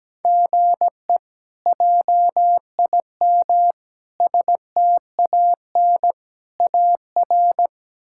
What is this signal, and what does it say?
Morse code, heard as a keyed tone, about 17 wpm, 700 Hz. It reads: GE JIM STAN AR
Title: K1USN SST QSO Practice - 17wpm